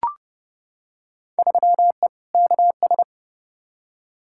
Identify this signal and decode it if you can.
Morse code, heard as a keyed tone, about 30 wpm, 700 Hz. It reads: 3E KS
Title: Field Day Exchanges Encore - 30wpm